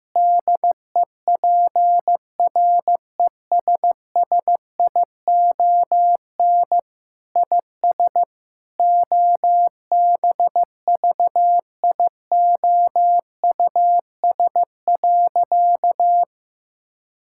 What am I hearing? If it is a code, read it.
Morse code, heard as a keyed tone, about 15 wpm, 700 Hz. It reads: DEPRESSION IS OBVIOUS.
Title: Sentences from Top 1600 Words 15wpm